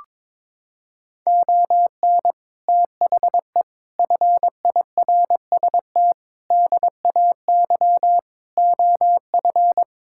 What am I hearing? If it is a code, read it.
Morse code, heard as a keyed tone, about 21 wpm, 700 Hz. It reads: ON THE FIRST DAY OF
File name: practice / 22wpm / Sentences with 8-Word Phrases - 22wpm